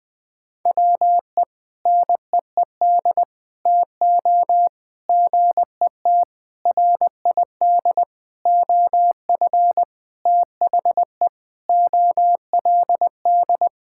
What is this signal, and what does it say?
Morse code, heard as a keyed tone, about 20 wpm, 700 Hz. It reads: WE NEED TO GET RID OF THE OLD
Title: Sentences with 4-Word Phrases - 20wpm